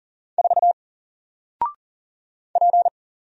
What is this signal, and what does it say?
Morse code, heard as a keyed tone, about 40 wpm, 700 Hz. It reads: 4 P
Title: Single Character (Letter:Number:Punctuation:Prosign) Alt 40wpm